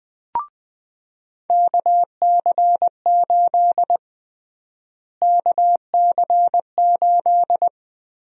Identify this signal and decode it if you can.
Morse code, heard as a keyed tone, about 20 wpm, 700 Hz. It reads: KC8 KC8
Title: USA Call Sign Prefixes - Repeated 2x - 20wpm